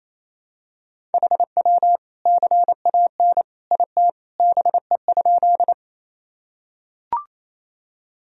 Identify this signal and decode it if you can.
Morse code, heard as a keyed tone, about 27 wpm, 700 Hz. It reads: HW CAN IT BE?
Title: Sentences from Top 100 Words and Abbreviations 28wpm